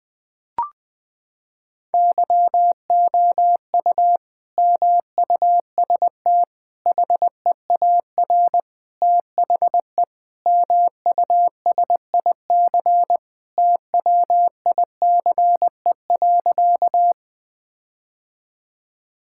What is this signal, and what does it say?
Morse code, heard as a keyed tone, about 20 wpm, 700 Hz. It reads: YOU MUST HEAR THE MUSIC TWICE.
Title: Sentences from Top 1600 Words 20wpm